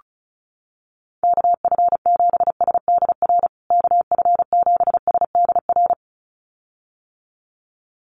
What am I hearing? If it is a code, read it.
Morse code, heard as a keyed tone, about 34 wpm, 700 Hz. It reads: KF7SDR KF7SDR
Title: USA Call Signs Repeated 2x 35wpm